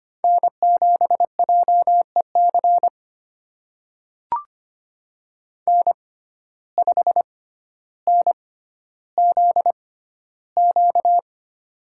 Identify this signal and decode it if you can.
Morse code, heard as a keyed tone, about 25 wpm, spaced as for 13 wpm, 700 Hz. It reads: N7JEC N5NZQ
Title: USA Call Signs - 5 Characters - 6x Character Spacing - 25wpm